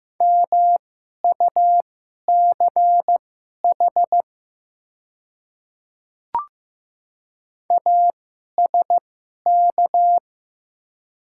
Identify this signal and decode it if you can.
Morse code, heard as a keyed tone, about 15 wpm, spaced as for 11 wpm, 700 Hz. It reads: MUCH ASK
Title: 2x Character Spacing - Top 200 Words - 15wpm